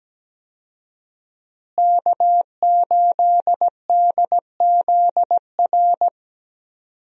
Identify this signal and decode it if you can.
Morse code, heard as a keyed tone, about 17 wpm, 700 Hz. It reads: K8DZR
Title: USA Call Signs - 5 Characters - Repeated 2x - 17wpm